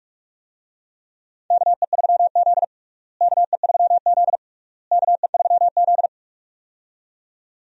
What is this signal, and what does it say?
Morse code, heard as a keyed tone, about 46 wpm, 700 Hz. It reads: KE3Z KE3Z KE3Z
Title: USA Call Signs - 4 Characters - Repeated 3x - 45wpm